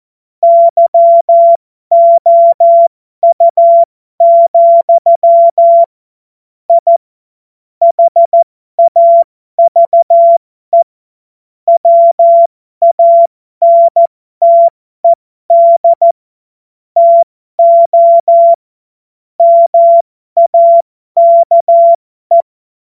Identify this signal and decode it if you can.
Morse code, heard as a keyed tone, about 14 wpm, spaced as for 12 wpm, 680 Hz. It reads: YOU, I HAVE WANTED TO MAKE
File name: Morsekurs / Witze